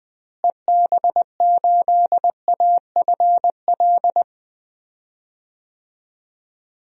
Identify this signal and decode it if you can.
Morse code, heard as a keyed tone, about 20 wpm, 700 Hz. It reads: EB8AFL
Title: DXCC-100 Call Signs - 6 Characters - Repeated 2x - 20wpm